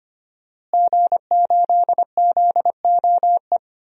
Morse code, heard as a keyed tone, about 25 wpm, 700 Hz. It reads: G8ZOE
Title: DXCC-100 Call Signs - 5 Characters - Repeated 2x - 25wpm